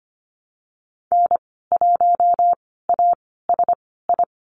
Morse code, heard as a keyed tone, about 25 wpm, spaced as for 16 wpm, 700 Hz. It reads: N1ASI
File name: Lesson 15g - Number 5 - Call Signs 25wpm